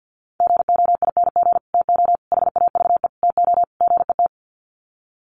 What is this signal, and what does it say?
Morse code, heard as a keyed tone, about 50 wpm, 700 Hz. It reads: GOING TO HAVE TO GET
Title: Sets of 5 Words Encore2 50wpm